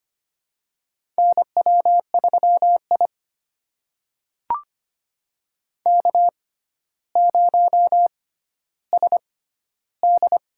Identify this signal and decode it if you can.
Morse code, heard as a keyed tone, about 25 wpm, spaced as for 13 wpm, 700 Hz. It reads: NW3I K0SD
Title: USA Call Signs - 4 Characters - 6x Character Spacing - 25wpm